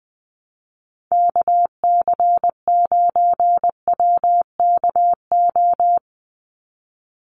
Morse code, heard as a keyed tone, about 20 wpm, 700 Hz. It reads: KC9WKO
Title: USA Call Signs 20wpm